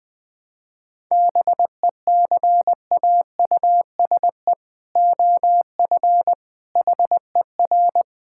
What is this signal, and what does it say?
Morse code, heard as a keyed tone, about 20 wpm, 700 Hz. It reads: BECAUSE OF HER
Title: Sentences from Top 1000 Words - Review Part 5 - 20wpm